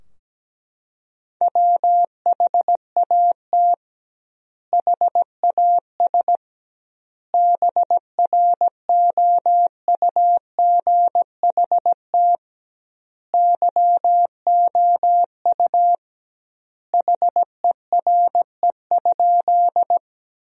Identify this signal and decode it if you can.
Morse code, heard as a keyed tone, about 17 wpm, 700 Hz. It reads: WHAT HAS BROUGHT YOU HERE?
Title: 2x Word-Spacing - Sentences from Top 500 Words - 17wpm